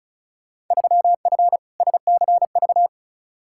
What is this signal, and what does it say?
Morse code, heard as a keyed tone, about 34 wpm, 700 Hz. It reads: 3F SCV
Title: Field Day Exchanges - 35wpm